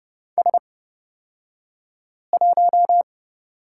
Morse code, heard as a keyed tone, about 30 wpm, spaced as for 5 wpm, 700 Hz. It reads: S1
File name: Lesson 38d - Letter q - Two Characters 30wpm